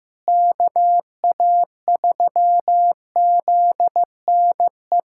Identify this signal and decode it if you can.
Morse code, heard as a keyed tone, about 16 wpm, 700 Hz. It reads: KA3ZNE
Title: USA Callsign DE Callsign - 6 Characters - 15wpm